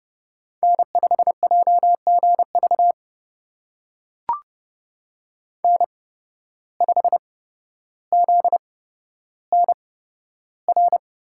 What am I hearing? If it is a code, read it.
Morse code, heard as a keyed tone, about 30 wpm, spaced as for 13 wpm, 700 Hz. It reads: N5JGV N5ZNR
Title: USA Call Signs - 5 Characters - 8x Character Spacing - 30wpm